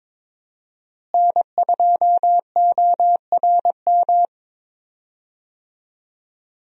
Morse code, heard as a keyed tone, about 21 wpm, 700 Hz. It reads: N2ORM
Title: USA Call Signs - 5 Characters - Repeated 3x - 22wpm